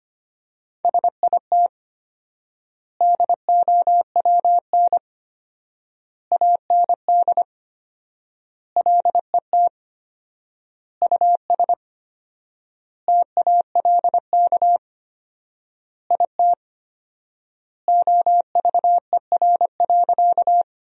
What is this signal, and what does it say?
Morse code, heard as a keyed tone, about 25 wpm, 700 Hz. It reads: SIT DOWN AND LET US TALK IT OVER.
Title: 4x Word-Spacing - Sentences from Top 500 Words - 25wpm